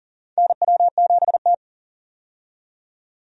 Morse code, heard as a keyed tone, about 39 wpm, 700 Hz. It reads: NW7T
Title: USA Call Signs - 4 Characters - 40wpm